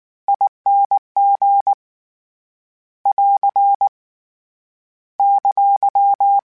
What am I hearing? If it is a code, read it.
Morse code, heard as a keyed tone, about 19 wpm, 800 Hz. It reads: ING <AR> !
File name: Variable Spacing / Variable Spacing-mp3s / T AOEs @18